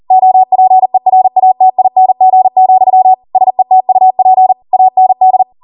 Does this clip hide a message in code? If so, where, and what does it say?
Yes: Morse code from 0.1 to 5.4 s, heard as a keyed tone, about 40 wpm, 750 Hz. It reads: OPERATING, SETUP AND